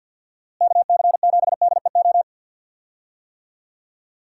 Morse code, heard as a keyed tone, about 52 wpm, 700 Hz. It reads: KC7DEO
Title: USA Call Signs - 6 Characters - Repeated 2x - 50wpm